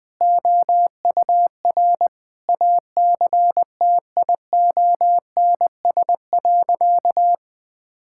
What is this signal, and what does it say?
Morse code, heard as a keyed tone, about 20 wpm, 700 Hz. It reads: OUR ACTIONS.
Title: Sentences from Top 1000 Words - Review Part 6 - 20wpm